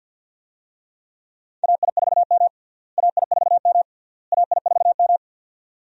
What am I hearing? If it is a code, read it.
Morse code, heard as a keyed tone, about 52 wpm, 700 Hz. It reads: AI4M AI4M AI4M
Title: USA Call Signs - 4 Characters - Repeated 3x - 50wpm